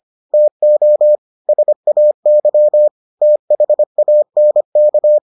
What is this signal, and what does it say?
Morse code, heard as a keyed tone, about 25 wpm, 600 Hz. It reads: TO SAY THANK